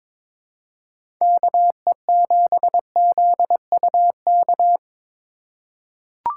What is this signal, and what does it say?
Morse code, heard as a keyed tone, about 23 wpm, 700 Hz. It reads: KE7ZUK
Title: USA Call Signs - 3x Character Spacing - 22wpm